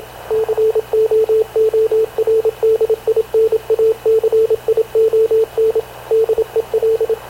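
Morse code, heard as a keyed tone, about 27 wpm, 450 Hz, with radio noise. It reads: COORDINACION DEL
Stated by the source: Havana Radio / CLA sending Spanish press (PX) in Morse, recorded by Radio Officer Paul Dumesnil onboard CNAV QUEST/CZDO.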